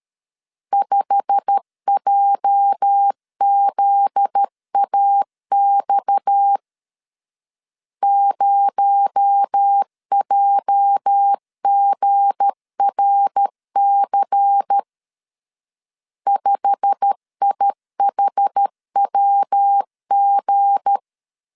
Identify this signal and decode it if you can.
Morse code, heard as a keyed tone, about 13 wpm, 800 Hz. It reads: 5JZAX 0JGRC 5IHWG
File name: LEZIONI / 8 YJ45 / 8 YJ45 12wpm 2151